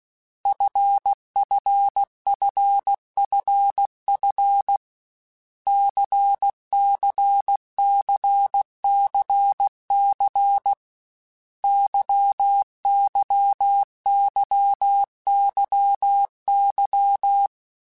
Morse code, heard as a keyed tone, about 16 wpm, 800 Hz. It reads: FFFFF CCCCC YYYYY
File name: Basic Letter Groups By7Groups / LFCYQ / LFCYQ@13